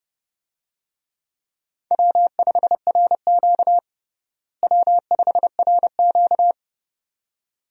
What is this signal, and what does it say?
Morse code, heard as a keyed tone, about 32 wpm, 700 Hz. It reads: W5RQ W5RQ
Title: USA Call Signs - 4 Characters - Repeated 2x - 30wpm